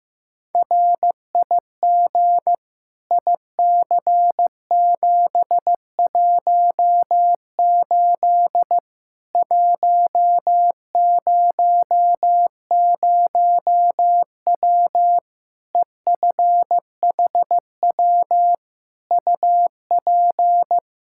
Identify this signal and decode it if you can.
Morse code, heard as a keyed tone, about 15 wpm, 700 Hz. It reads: RIG IC718 100W EFHW UP
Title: Practice QSO - Exchange 2 - Info - 15wpm